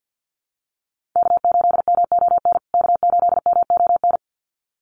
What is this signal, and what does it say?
Morse code, heard as a keyed tone, about 50 wpm, 700 Hz. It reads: K8MON K8MON
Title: USA Call Signs Repeated 2x 50wpm